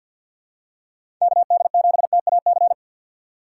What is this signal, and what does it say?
Morse code, heard as a keyed tone, about 50 wpm, 700 Hz. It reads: KD7TAC
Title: USA Call Signs - 6 Characters - 3x Character Spacing - 50wpm